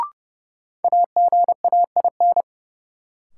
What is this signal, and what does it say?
Morse code, heard as a keyed tone, about 30 wpm, 700 Hz. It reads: AGAIN
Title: Sets of 4 Words - New Word Drill - 30wpm